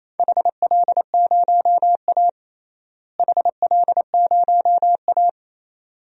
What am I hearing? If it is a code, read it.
Morse code, heard as a keyed tone, about 28 wpm, 700 Hz. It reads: HL0A HL0A
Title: DXCC-100 Call Signs - 4 Characters - Repeated 2x - 28wpm